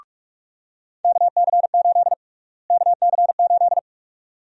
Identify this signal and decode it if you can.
Morse code, heard as a keyed tone, about 46 wpm, 700 Hz. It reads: KC8 KC8
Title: USA Call Sign Prefixes - Repeated 2x - 45wpm